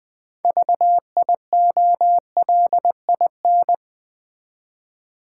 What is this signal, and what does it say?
Morse code, heard as a keyed tone, about 20 wpm, 700 Hz. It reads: VIOLIN
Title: Easy Words with Hints - 20wpm